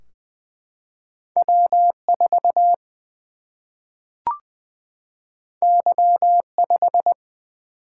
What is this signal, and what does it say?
Morse code, heard as a keyed tone, about 21 wpm, 700 Hz. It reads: W 4 Y 5
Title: Single-Letter Single-Number 20wpm